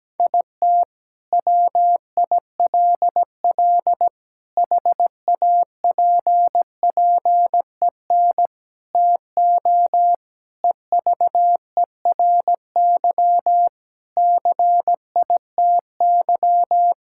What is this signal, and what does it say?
Morse code, heard as a keyed tone, about 17 wpm, 700 Hz. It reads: IT WILL HAPPEN TO EVERY CITY
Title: Sentences from Top 1000 Words - Review Part 6 - 17wpm